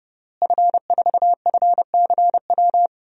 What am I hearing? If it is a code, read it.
Morse code, heard as a keyed tone, about 30 wpm, 700 Hz. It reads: F4FCW
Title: DXCC-100 Call Signs - 5 Characters - Repeated 3x - 30wpm